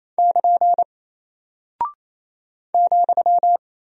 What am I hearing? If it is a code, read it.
Morse code, heard as a keyed tone, about 28 wpm, 700 Hz. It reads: <KN> ,
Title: Single Character (Letter:Number:Punctuation:Prosign) Alt 28wpm